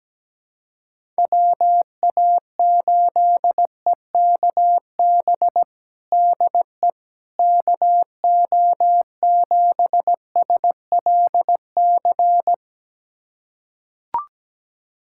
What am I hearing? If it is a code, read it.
Morse code, heard as a keyed tone, about 17 wpm, 700 Hz. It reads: WA8EKB DE KO7SLC
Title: USA Callsign DE Callsign - 6 Characters - 17wpm